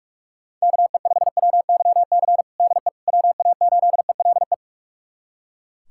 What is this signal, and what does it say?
Morse code, heard as a keyed tone, about 43 wpm, 700 Hz. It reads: KE5WYC DE WA8ELE
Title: USA Callsign DE Callsign - 6 Characters - 45wpm